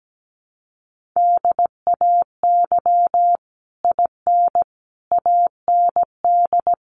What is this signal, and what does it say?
Morse code, heard as a keyed tone, about 17 wpm, 700 Hz. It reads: DAY IN AND
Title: Sets of 5 Words Encore2 17wpm